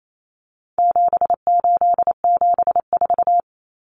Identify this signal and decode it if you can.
Morse code, heard as a keyed tone, about 28 wpm, 700 Hz. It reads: 7874
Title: Common Cut Numbers - 4-Digit 28wpm